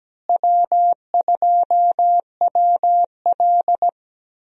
Morse code, heard as a keyed tone, about 17 wpm, 700 Hz. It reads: W2WL
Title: USA Call Signs - 4 Characters - Repeated 3x - 17wpm